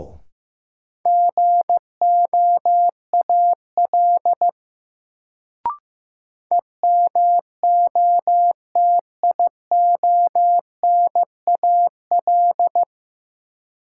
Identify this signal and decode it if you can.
Morse code, heard as a keyed tone, about 15 wpm, 700 Hz. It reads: GOAL EMOTIONAL
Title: Sentences from Top 1000 Words - Review Part 5 - New Word Drill - 15wpm